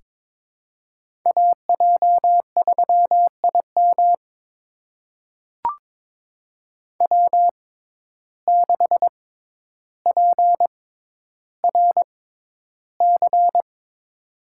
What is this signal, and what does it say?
Morse code, heard as a keyed tone, about 22 wpm, spaced as for 11 wpm, 700 Hz. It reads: AJ3IM W6PRC
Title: USA Call Signs - 5 Characters - 6x Character Spacing - 22wpm